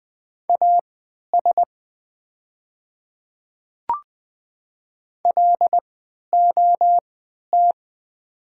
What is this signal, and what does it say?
Morse code, heard as a keyed tone, about 20 wpm, spaced as for 11 wpm, 700 Hz. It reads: AS LOT
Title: 3x Character Spacing - Top 500 Words - 20wpm